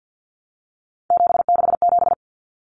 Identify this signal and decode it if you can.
Morse code, heard as a keyed tone, about 50 wpm, 700 Hz. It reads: 767
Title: Numbers - 3-Digit 50wpm